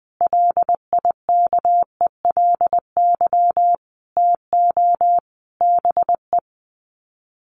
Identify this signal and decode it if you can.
Morse code, heard as a keyed tone, about 20 wpm, 700 Hz. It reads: LIKELY TO BE
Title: Sets of 5 Words 20wpm